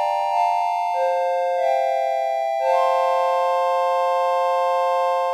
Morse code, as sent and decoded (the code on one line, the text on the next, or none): none